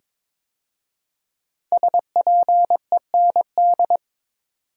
Morse code, heard as a keyed tone, about 23 wpm, 700 Hz. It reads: SPEND
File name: 5 Letter Words - First 3 Letters Given - 22wpm